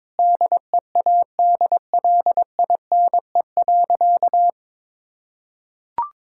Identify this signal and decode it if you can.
Morse code, heard as a keyed tone, about 22 wpm, 700 Hz. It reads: DEADLINE.
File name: practice / 22wpm / Sentences with 8-Word Phrases - 22wpm